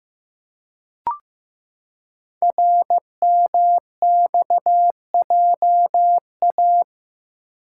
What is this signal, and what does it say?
Morse code, heard as a keyed tone, about 15 wpm, 700 Hz. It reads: RMXJA
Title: Code Groups - 5 Letters - 15wpm